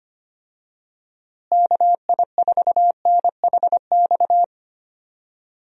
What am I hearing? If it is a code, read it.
Morse code, heard as a keyed tone, about 25 wpm, 700 Hz. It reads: KI4NHX
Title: USA Call Signs - 6 Characters - Repeated 2x - 25wpm